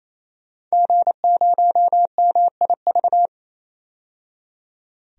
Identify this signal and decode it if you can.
Morse code, heard as a keyed tone, about 28 wpm, 690 Hz. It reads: G0MIV
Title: DXCC-100 Call Signs - 5 Characters - 28wpm